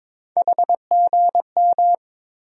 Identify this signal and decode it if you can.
Morse code, heard as a keyed tone, about 21 wpm, 700 Hz. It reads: HGM
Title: USA Call Sign Postfixes - 3 Letters 22wpm